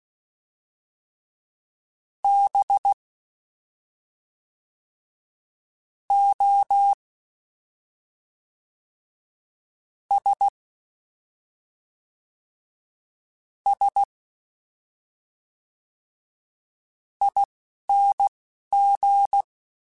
Morse code, heard as a keyed tone, about 16 wpm, spaced as for 4 wpm, 800 Hz. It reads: BOSSING
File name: Special INGs / Special INGs-mp3s / Fast ing @10